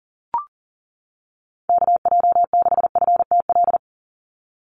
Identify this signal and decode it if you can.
Morse code, heard as a keyed tone, about 37 wpm, 690 Hz. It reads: KJ6FTL
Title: USA Call Signs 40wpm